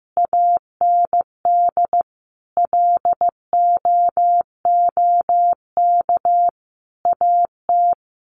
Morse code, heard as a keyed tone, about 15 wpm, 700 Hz. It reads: AND LOOK AT
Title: Sentences from Top 1400 Words 15wpm